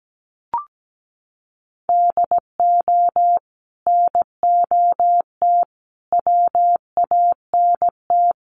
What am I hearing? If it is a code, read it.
Morse code, heard as a keyed tone, about 17 wpm, 700 Hz. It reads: DO NOT WANT
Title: Sets of 5 Words 17wpm